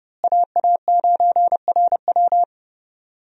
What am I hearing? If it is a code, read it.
Morse code, heard as a keyed tone, about 30 wpm, 700 Hz. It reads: AA9RW
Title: USA Call Signs - 5 Characters - Repeated 3x - 30wpm